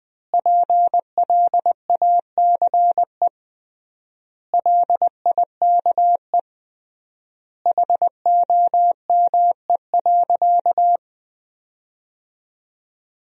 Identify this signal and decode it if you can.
Morse code, heard as a keyed tone, about 21 wpm, 700 Hz. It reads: PLACE LIKE HOME.
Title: 3x Word-Spacing - Sentences from Top 200 Words - 20wpm